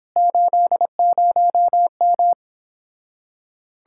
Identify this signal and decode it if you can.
Morse code, heard as a keyed tone, about 26 wpm, 700 Hz. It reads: 80M
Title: Speed-Racing Common QSO Phrases - 22wpm